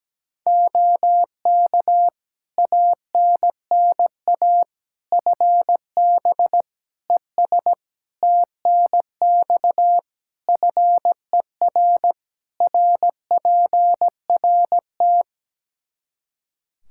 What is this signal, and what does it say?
Morse code, heard as a keyed tone, about 17 wpm, 700 Hz. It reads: OK ANNA FB ES TNX FER RPRT
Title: Practice QSO - Exchange 2 - Info - 17wpm